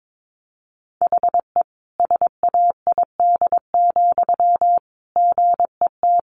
Morse code, heard as a keyed tone, about 22 wpm, 700 Hz. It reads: HE SAID, GET
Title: Sentences from Top 400 Words 22wpm